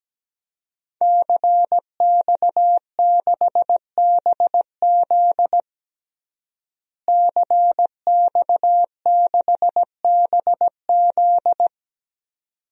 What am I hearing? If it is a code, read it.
Morse code, heard as a keyed tone, about 17 wpm, 700 Hz. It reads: CX6BZ CX6BZ
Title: DXCC-100 Call Signs - 5 Characters - Repeated 2x - 17wpm